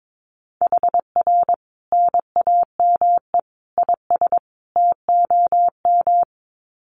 Morse code, heard as a keyed tone, about 22 wpm, 700 Hz. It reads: HR NAME IS TOM
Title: Common QSO Phrases 22wpm